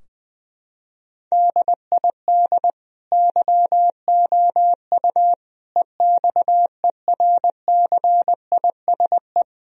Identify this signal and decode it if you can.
Morse code, heard as a keyed tone, about 20 wpm, 700 Hz. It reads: DID YOU EXERCISE
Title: Sentences from Top 1000 Words - Review Part 4 - 20wpm